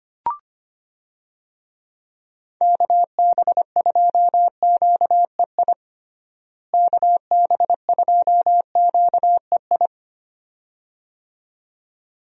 Morse code, heard as a keyed tone, about 25 wpm, 700 Hz. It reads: KB2QEI KB2QEI
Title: USA Call Signs - 6 Characters - Repeated 2x - 25wpm